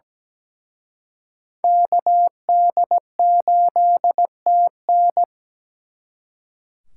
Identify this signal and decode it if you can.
Morse code, heard as a keyed tone, about 20 wpm, 700 Hz. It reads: KD8TN
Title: Speed Racing - USA Call Signs 17wpm